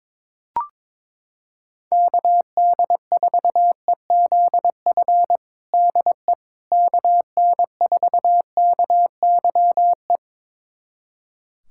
Morse code, heard as a keyed tone, about 22 wpm, 700 Hz. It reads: KD4EZF DE KN4KYE
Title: USA Callsign DE Callsign 22wpm